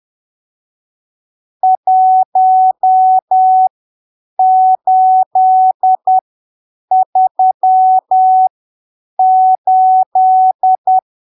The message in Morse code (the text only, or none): 1838